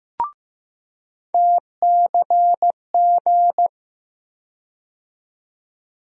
Morse code, heard as a keyed tone, about 15 wpm, 700 Hz. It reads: TCG
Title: USA Call Sign Postfixes - 3 Letters 15wpm